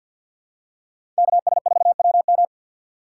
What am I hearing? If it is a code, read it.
Morse code, heard as a keyed tone, about 48 wpm, 700 Hz. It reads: KS4WM
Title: USA Call Signs - 5 Characters - 6x Character Spacing - 50wpm